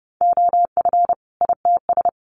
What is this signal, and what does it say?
Morse code, heard as a keyed tone, about 30 wpm, 700 Hz. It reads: OF ITS
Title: Sets of 2 Words 30wpm